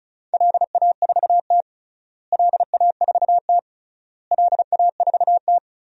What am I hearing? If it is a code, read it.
Morse code, heard as a keyed tone, about 36 wpm, 690 Hz. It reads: LA4T LA4T LA4T
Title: DXCC-100 Call Signs - 4 Characters - Repeated 3x - 35wpm